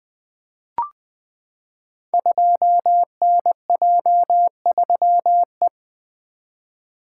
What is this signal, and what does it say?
Morse code, heard as a keyed tone, about 20 wpm, 700 Hz. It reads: 2NJ3E
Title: Code Groups - 5 Letter:Numbers - 20wpm